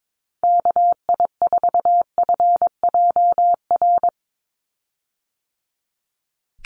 Morse code, heard as a keyed tone, about 22 wpm, 700 Hz. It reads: KI4FJR
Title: USA Call Signs 22wpm